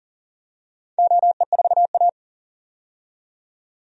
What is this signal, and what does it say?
Morse code, heard as a keyed tone, about 40 wpm, 700 Hz. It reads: OE4A
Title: DXCC-100 Call Signs - 4 Characters - 40wpm